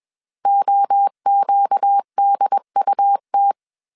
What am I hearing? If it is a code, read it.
Morse code, heard as a keyed tone, about 21 wpm, 800 Hz. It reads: OQDUT
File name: CW / DET / 1 6 1500C S A 20wpm 2048